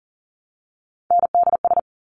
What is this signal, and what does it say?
Morse code, heard as a keyed tone, about 38 wpm, 700 Hz. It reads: NDS
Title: Top Three Letter Combinations in Words 40wpm